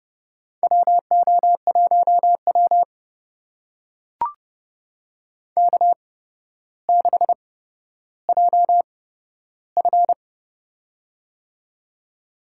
Morse code, heard as a keyed tone, about 32 wpm, spaced as for 13 wpm, 700 Hz. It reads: WO1W K6JF
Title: USA Call Signs - 4 Characters - 8x Character Spacing - 30wpm